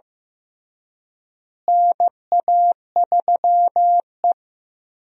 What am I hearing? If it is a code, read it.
Morse code, heard as a keyed tone, about 15 wpm, 700 Hz. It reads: NA3E
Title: USA Call Signs - 4 Characters - Repeated 3x - 15wpm